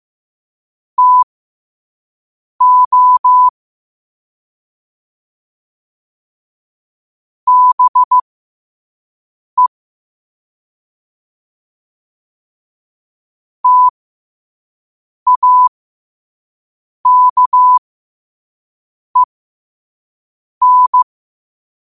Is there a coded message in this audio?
Yes: Morse code from 1.0 to 21.0 s, heard as a keyed tone, about 15 wpm, spaced as for 5 wpm, 1000 Hz. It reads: TO BE TAKEN